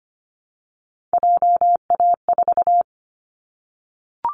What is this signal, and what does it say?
Morse code, heard as a keyed tone, about 25 wpm, 700 Hz. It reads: JA4
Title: Callsign Prefixes for Top 25 DXCC - Repeated 3x - 25wpm